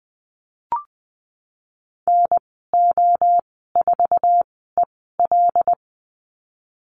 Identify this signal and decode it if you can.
Morse code, heard as a keyed tone, about 20 wpm, spaced as for 15 wpm, 700 Hz. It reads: NO4EL
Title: 147 Lesson 24g - Letter y - Call Signs 20wpm